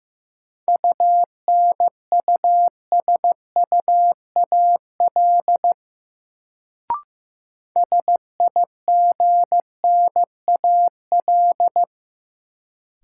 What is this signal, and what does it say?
Morse code, heard as a keyed tone, about 15 wpm, 700 Hz. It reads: UNUSUAL SIGNAL
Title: Sentences from Top 1400 Words - New Word Drill - 15wpm